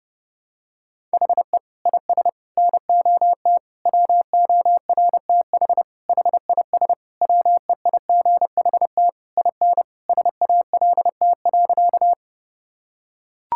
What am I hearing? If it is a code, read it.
Morse code, heard as a keyed tone, about 30 wpm, 700 Hz. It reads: HE IS NOT WORTH HIS WEIGHT IN SALT.
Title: Sentences from Top 1000 Words - Review Part 5 - 30wpm